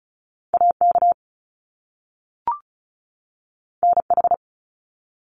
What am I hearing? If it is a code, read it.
Morse code, heard as a keyed tone, about 36 wpm, 700 Hz. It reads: AK NH
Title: US States 35wpm